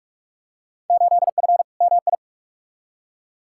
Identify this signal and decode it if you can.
Morse code, heard as a keyed tone, about 42 wpm, 700 Hz. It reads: 8F MI
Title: Field Day Exchanges Encore - 45wpm